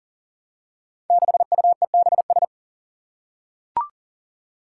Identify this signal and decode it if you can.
Morse code, heard as a keyed tone, about 40 wpm, 700 Hz. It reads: 6UEBS
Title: Five Challenging Letter-Number Combinations 40wpm